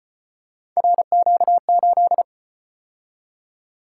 Morse code, heard as a keyed tone, about 35 wpm, 700 Hz. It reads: RQ8